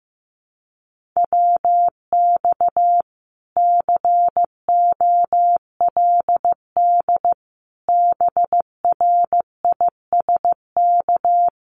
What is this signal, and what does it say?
Morse code, heard as a keyed tone, about 15 wpm, 700 Hz. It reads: WX COLD BRISK
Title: Common QSO Phrases Encore 15wpm